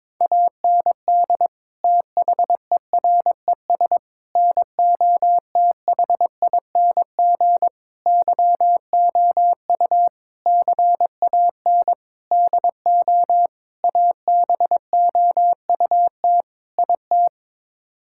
Morse code, heard as a keyed tone, about 22 wpm, 700 Hz. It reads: AND THERES NOTHING YOU CAN DO ABOUT IT
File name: practice / 22wpm / Sets of 8 Words - 22wpm